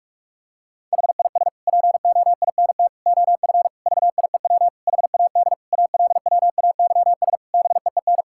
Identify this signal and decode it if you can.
Morse code, heard as a keyed tone, about 46 wpm, 700 Hz. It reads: HIS POINT OF VIEW HAD ALWAYS BEEN
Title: Sentences from Top 1000 Words - Review Part 6 - 45wpm